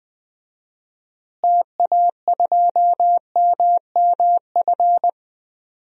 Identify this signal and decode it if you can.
Morse code, heard as a keyed tone, about 21 wpm, 700 Hz. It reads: TA2MMF
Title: DXCC-100 Call Signs - 6 Characters - Repeated 3x - 20wpm